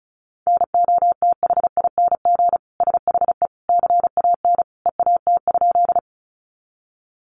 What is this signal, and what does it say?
Morse code, heard as a keyed tone, about 34 wpm, 700 Hz. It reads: NOTHING SHE CAN EAT?
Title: Sentences from Top 500 Words 35wpm